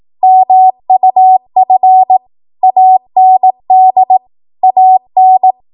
Morse code, heard as a keyed tone, about 18 wpm, 750 Hz. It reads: MUF AND AN
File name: Archive / 18 WPM / 18WPM